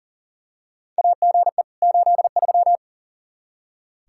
Morse code, heard as a keyed tone, about 40 wpm, 700 Hz. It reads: AGE 83
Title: Practice QSO - Exchange 3 - More Info - 40wpm